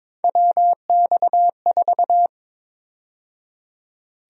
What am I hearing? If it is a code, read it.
Morse code, heard as a keyed tone, about 22 wpm, 700 Hz. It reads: WX4
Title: USA Call Sign Prefixes - 22wpm